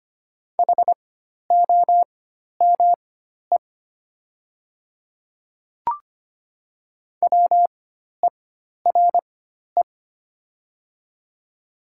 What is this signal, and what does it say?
Morse code, heard as a keyed tone, about 25 wpm, spaced as for 12 wpm, 700 Hz. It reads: HOME WERE
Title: 4x Character Spacing - Top 200 Words - 25wpm